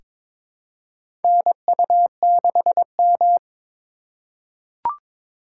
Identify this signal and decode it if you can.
Morse code, heard as a keyed tone, about 22 wpm, 700 Hz. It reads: NU6M
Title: USA Call Signs - 4 Characters - 4x Character Spacing - 22wpm